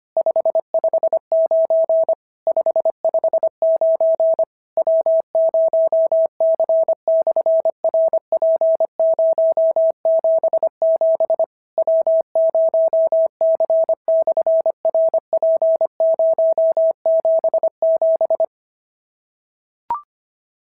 Morse code, heard as a keyed tone, about 25 wpm, 650 Hz. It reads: 559 559 W0C/RP077 W0C/RP077
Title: SOTA QSO Practice 25wpm